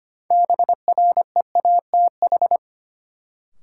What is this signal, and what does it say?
Morse code, heard as a keyed tone, about 25 wpm, 700 Hz. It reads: BREATH
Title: Sets of 5 Words Encore - New Word Drill - 25wpm